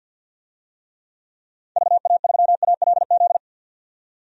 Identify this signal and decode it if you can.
Morse code, heard as a keyed tone, about 46 wpm, 700 Hz. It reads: VA3ALZ
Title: DXCC-100 Call Signs - 6 Characters - 50wpm